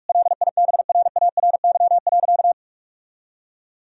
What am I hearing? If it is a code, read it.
Morse code, heard as a keyed tone, about 46 wpm, 700 Hz. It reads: LIBRARY.
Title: Sentences from Top 1000 Words - Review Part 4 - 45wpm